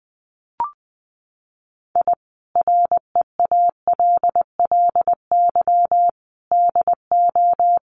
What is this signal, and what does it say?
Morse code, heard as a keyed tone, about 20 wpm, 700 Hz. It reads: I REALLY DO
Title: Sets of 5 Words Encore2 20wpm